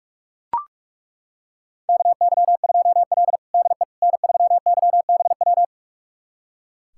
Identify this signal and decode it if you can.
Morse code, heard as a keyed tone, about 42 wpm, 700 Hz. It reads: KY2L DE N3YBW
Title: USA Callsign DE Callsign 45wpm